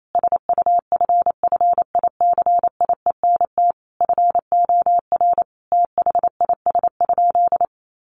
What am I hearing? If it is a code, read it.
Morse code, heard as a keyed tone, about 28 wpm, 700 Hz. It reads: SUFFICIENT FOR THIS?
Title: Sentences from Top 1400 Words 28wpm